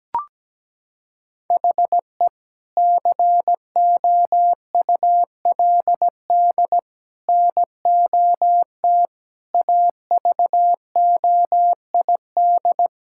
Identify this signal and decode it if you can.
Morse code, heard as a keyed tone, about 17 wpm, 700 Hz. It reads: HE COULD NOT AVOID
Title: Sentences from Top 1000 Words - Review Part 6 - 17wpm